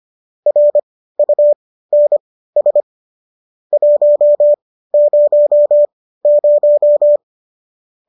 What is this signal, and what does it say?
Morse code, heard as a keyed tone, about 25 wpm, spaced as for 15 wpm, 590 Hz. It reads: RUNS 100